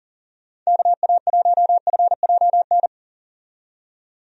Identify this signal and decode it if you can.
Morse code, heard as a keyed tone, about 40 wpm, 700 Hz. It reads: KA1FJN
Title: USA Call Signs - 6 Characters - 40wpm